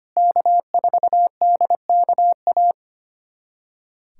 Morse code, heard as a keyed tone, about 25 wpm, 700 Hz. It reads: K4DKA
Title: USA Callsign DE Callsign - 5 Characters - 25wpm